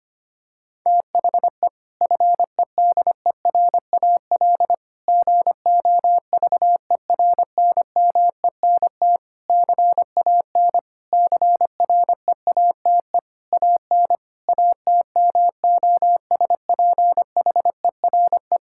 Morse code, heard as a keyed tone, about 25 wpm, 700 Hz. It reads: THE FEDERAL GOVERNMENT CAN CREATE AN ATMOSPHERE
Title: Sentences from Top 1600 Words 25wpm